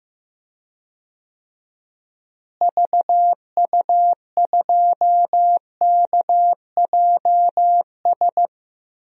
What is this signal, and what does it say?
Morse code, heard as a keyed tone, about 15 wpm, 700 Hz. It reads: VU2KJS
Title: DXCC-100 Call Signs - 6 Characters - Repeated 3x - 15wpm